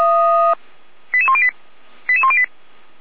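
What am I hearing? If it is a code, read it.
Five-tone selective call (ZVEI), heard as quick runs of five tones, tones of about 70 ms. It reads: 88108 88108
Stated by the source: ZVEI five-tone selective call received by SDR.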